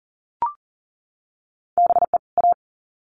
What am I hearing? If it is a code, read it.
Morse code, heard as a keyed tone, about 40 wpm, 700 Hz. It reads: BE A
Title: Sets of 2 Words 40wpm